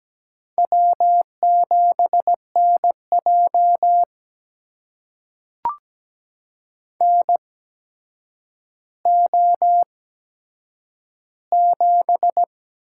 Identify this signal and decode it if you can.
Morse code, heard as a keyed tone, about 17 wpm, spaced as for 8 wpm, 700 Hz. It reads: W7NJ NO7
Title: USA Call Signs - 4 Characters - 8x Character Spacing - 17wpm